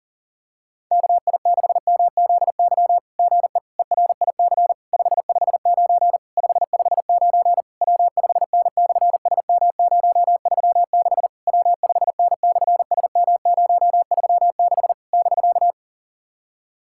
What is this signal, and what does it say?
Morse code, heard as a keyed tone, about 40 wpm, 690 Hz. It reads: KI6MZY GE ERIC 559 559 W5N/SM036 W5N/SM036 <BK>
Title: SOTA QSO Practice 40wpm